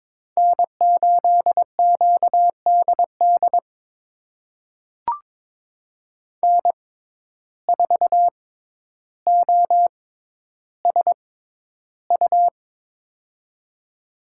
Morse code, heard as a keyed tone, about 21 wpm, spaced as for 11 wpm, 700 Hz. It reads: N8QDD N4OSU
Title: USA Call Signs - 5 Characters - 6x Character Spacing - 22wpm